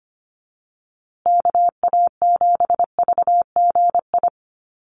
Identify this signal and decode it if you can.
Morse code, heard as a keyed tone, about 25 wpm, 700 Hz. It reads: KA7VGI
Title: USA Call Signs Repeated 2x 25wpm